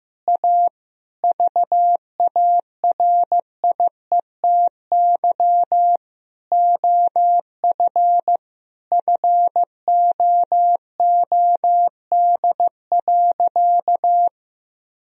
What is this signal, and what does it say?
Morse code, heard as a keyed tone, about 15 wpm, 700 Hz. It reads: A VARIETY OF FOOD.
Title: Sentences from Top 1000 Words - Review Part 4 - 15wpm